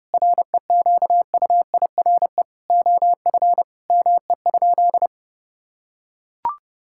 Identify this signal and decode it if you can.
Morse code, heard as a keyed tone, about 32 wpm, 700 Hz. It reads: REQUIRE OF ME?
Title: Sentences from Top 1000 Words - Review Part 6 - 30wpm